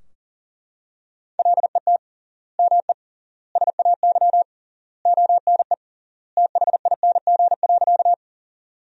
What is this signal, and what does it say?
Morse code, heard as a keyed tone, about 40 wpm, 700 Hz. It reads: LET ME SAY ONE THING.
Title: 3x Word-Spacing - Sentences from Top 300 Words - 40wpm